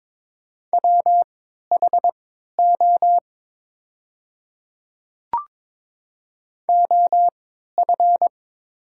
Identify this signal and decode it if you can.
Morse code, heard as a keyed tone, about 22 wpm, spaced as for 12 wpm, 700 Hz. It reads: WHO OF
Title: 3x Character Spacing - Top 100 Words - 22wpm